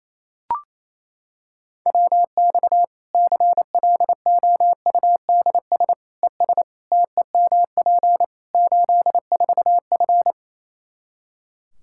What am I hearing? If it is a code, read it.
Morse code, heard as a keyed tone, about 28 wpm, 700 Hz. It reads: WX CLOUDS ES TEMP 84F
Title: Practice QSO - Exchange 2 - Info - 28wpm